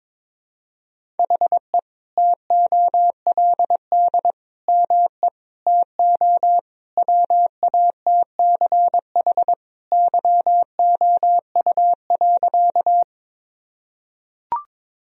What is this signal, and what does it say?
Morse code, heard as a keyed tone, about 22 wpm, 700 Hz. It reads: HE TOLD ME TO WATCH YOU.
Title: Sentences from Top 1000 Words - Review Part 1 - 22wpm